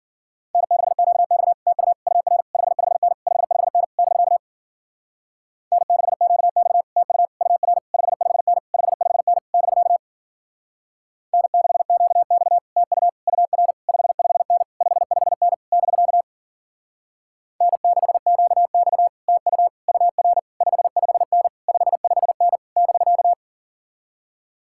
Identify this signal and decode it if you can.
Morse code, heard as a keyed tone, about 51 wpm, 700 Hz. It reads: N6QX TU UR 55N 55N <BK> N6QX TU UR 55N 55N <BK> N6QX TU UR 55N 55N <BK> N6QX TU UR 55N 55N <BK>